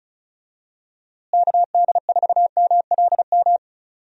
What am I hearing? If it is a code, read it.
Morse code, heard as a keyed tone, about 36 wpm, 700 Hz. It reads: KD4MLM
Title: USA Call Signs - 6 Characters - Repeated 2x - 35wpm